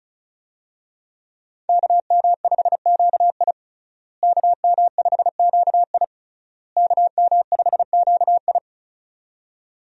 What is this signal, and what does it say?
Morse code, heard as a keyed tone, about 36 wpm, 700 Hz. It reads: KM5QI KM5QI KM5QI
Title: USA Call Signs - 5 Characters - Repeated 3x - 35wpm